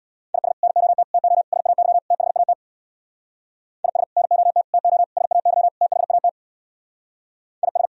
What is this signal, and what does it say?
Morse code, heard as a keyed tone, about 64 wpm, 690 Hz. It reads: IS NOT TO SAY THAT IS NOT TO SAY THAT IS